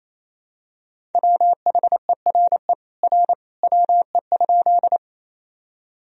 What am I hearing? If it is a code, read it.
Morse code, heard as a keyed tone, about 28 wpm, 700 Hz. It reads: WHERE R WE?
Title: Sentences from Top 200 Words and Abbreviations 28wpm